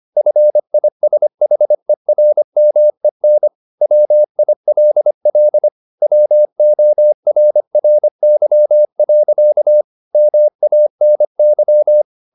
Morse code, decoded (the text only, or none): FISHERMEN WILL WORRY. MANY